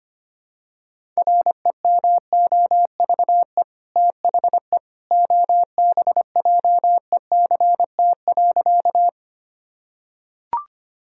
Morse code, heard as a keyed tone, about 25 wpm, 700 Hz. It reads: REMOVE THE OBJECT.
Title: Sentences from Top 1000 Words - Review Part 5 - 25wpm